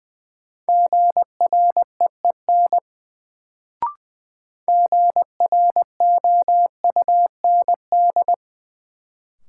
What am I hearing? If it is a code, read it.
Morse code, heard as a keyed tone, about 21 wpm, 700 Hz. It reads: GREEN GROUND
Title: Sentences from Top 500 Words - New Word Drill - 20wpm